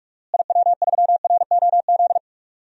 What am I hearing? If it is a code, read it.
Morse code, heard as a keyed tone, about 46 wpm, 700 Hz. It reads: IW3ROZ
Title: DXCC-100 Call Signs - 6 Characters - Repeated 2x - 45wpm